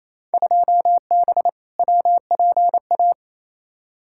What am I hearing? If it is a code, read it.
Morse code, heard as a keyed tone, about 28 wpm, 700 Hz. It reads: 2B WPA
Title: Field Day Exchanges Encore - 28wpm